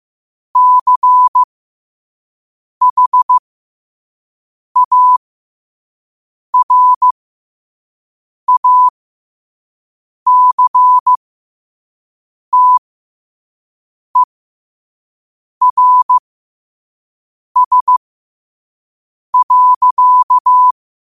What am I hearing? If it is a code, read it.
Morse code, heard as a keyed tone, about 15 wpm, spaced as for 5 wpm, 1000 Hz. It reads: CHARACTERS.
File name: qotdfiles / 5wpm